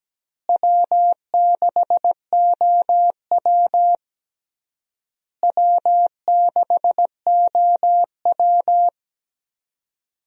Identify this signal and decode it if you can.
Morse code, heard as a keyed tone, about 17 wpm, 700 Hz. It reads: W6OW W6OW
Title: USA Call Signs - 4 Characters - Repeated 3x - 17wpm